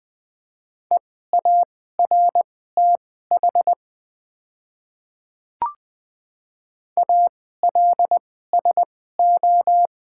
Morse code, heard as a keyed tone, about 20 wpm, spaced as for 14 wpm, 700 Hz. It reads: EARTH ALSO
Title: 2x Character Spacing - Top 300 Words - 20wpm